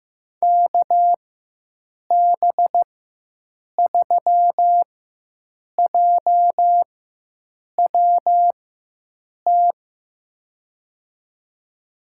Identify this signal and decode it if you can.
Morse code, heard as a keyed tone, about 15 wpm, spaced as for 7 wpm, 700 Hz. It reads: KB3JWT
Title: USA Call Signs - 6 Characters - 4x Character Spacing - 15wpm